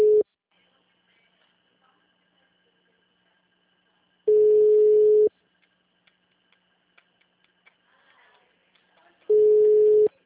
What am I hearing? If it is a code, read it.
Ringback tone (Europe, 425 Hz).